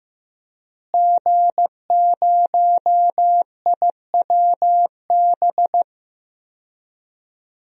Morse code, heard as a keyed tone, about 15 wpm, 700 Hz. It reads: G0IWB
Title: DXCC-100 Call Signs - 5 Characters - Repeated 2x - 15wpm